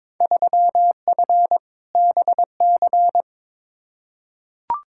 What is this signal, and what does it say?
Morse code, heard as a keyed tone, about 22 wpm, 700 Hz. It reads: 3F BC
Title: Field Day Exchanges Encore - 22wpm